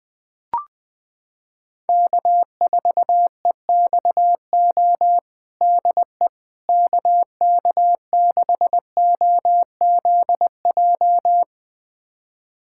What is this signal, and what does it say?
Morse code, heard as a keyed tone, about 20 wpm, 700 Hz. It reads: K4EXO DE KK6OZJ
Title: USA Callsign DE Callsign 20wpm